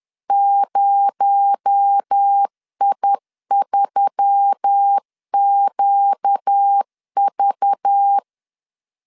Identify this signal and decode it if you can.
Morse code, heard as a keyed tone, about 10 wpm, 800 Hz. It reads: 0I3QV
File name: LEZIONI / 6 PKX0 / 6 PKX0 10wpm 2707